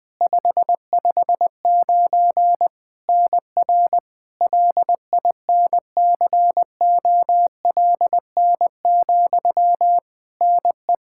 Morse code, heard as a keyed tone, about 20 wpm, 700 Hz. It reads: 559 NR LINCOLN, NE
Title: Practice QSO - Exchange 1 - Beginning - 20wpm